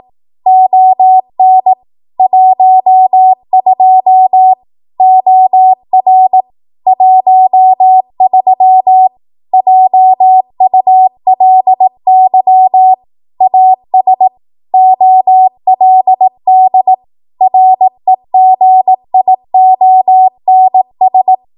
Morse code, heard as a keyed tone, about 18 wpm, 750 Hz. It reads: ON 12 OR 13 JULY AS OLD REGIONS
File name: Morse / 18WPM